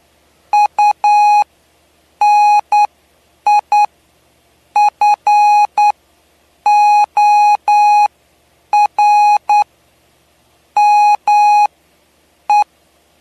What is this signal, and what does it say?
Morse code, heard as a keyed tone, about 9 wpm, spaced as for 7 wpm, 830 Hz. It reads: UNIFORME
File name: cw / py2lcd / 07-PY2LCD-Li��o 7